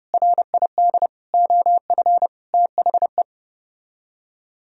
Morse code, heard as a keyed tone, about 32 wpm, 700 Hz. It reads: RID OF THE